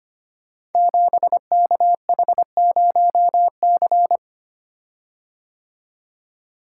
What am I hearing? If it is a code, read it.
Morse code, heard as a keyed tone, about 25 wpm, 700 Hz. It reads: 7KH0C
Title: Code Groups - 5 Letter:Numbers - 25wpm